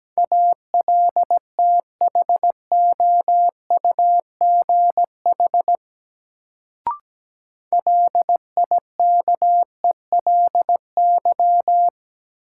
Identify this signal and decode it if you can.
Morse code, heard as a keyed tone, about 17 wpm, 700 Hz. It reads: ALTHOUGH LIKELY
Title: Sentences from Top 600 Words - New Word Drill - 17wpm